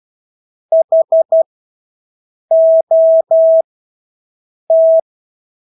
Morse code, heard as a keyed tone, about 12 wpm, spaced as for 6 wpm, 650 Hz. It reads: HOT
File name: Engelske ord 12-6 wpm